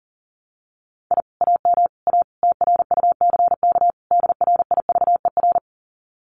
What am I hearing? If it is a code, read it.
Morse code, heard as a keyed tone, about 43 wpm, 700 Hz. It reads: I AM A TRUCK DRIVER
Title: Common QSO Phrases Encore 40wpm